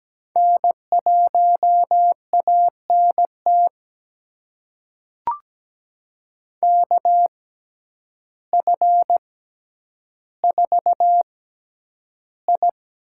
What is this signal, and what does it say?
Morse code, heard as a keyed tone, about 17 wpm, spaced as for 9 wpm, 700 Hz. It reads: N1ANT KF4I
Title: USA Call Signs - 5 Characters - 6x Character Spacing - 17wpm